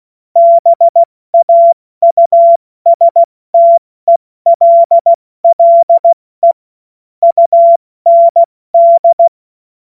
Morse code, heard as a keyed tone, about 16 wpm, 680 Hz. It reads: BAUSTELLE UND
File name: Morsekurs / Witze / Witz 27 Februar 2024 70 F80